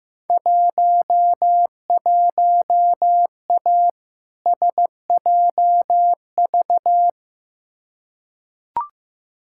Morse code, heard as a keyed tone, about 15 wpm, 700 Hz. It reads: 11A SJV
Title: Field Day Exchanges Encore - 15wpm